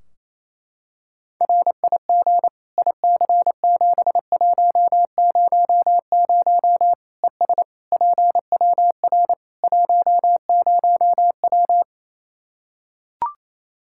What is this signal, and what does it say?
Morse code, heard as a keyed tone, about 28 wpm, 700 Hz. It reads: RIG IC7100 ES PWR 10W
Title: K1USN SST QSO Practice 28wpm